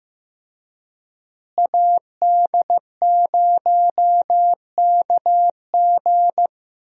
Morse code, heard as a keyed tone, about 15 wpm, 700 Hz. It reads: AD0KG
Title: USA Call Signs - 5 Characters - Repeated 3x - 15wpm